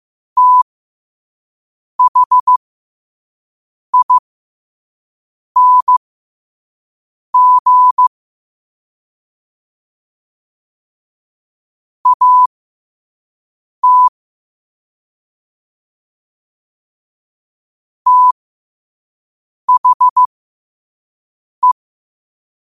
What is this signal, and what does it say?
Morse code, heard as a keyed tone, about 15 wpm, spaced as for 5 wpm, 1000 Hz. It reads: THING AT THE
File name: qotdfiles / 5wpm